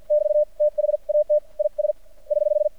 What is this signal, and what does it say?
Morse code, heard as a keyed tone, about 50 wpm, spaced as for 42 wpm, 600 Hz. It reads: <BT> THAT IS <HH>
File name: archived / CRSnet